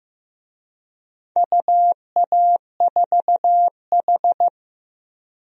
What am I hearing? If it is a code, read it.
Morse code, heard as a keyed tone, about 15 wpm, 700 Hz. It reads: UA4H
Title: DXCC-100 Call Signs - 4 Characters - 15wpm